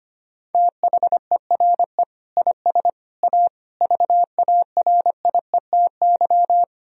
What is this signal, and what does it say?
Morse code, heard as a keyed tone, about 25 wpm, 700 Hz. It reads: THERE IS A VARIETY
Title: Sentences from Top 1000 Words - Review Part 4 - 25wpm